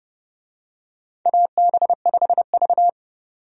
Morse code, heard as a keyed tone, about 30 wpm, 700 Hz. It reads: AB5V
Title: USA Call Signs - 4 Characters - 30wpm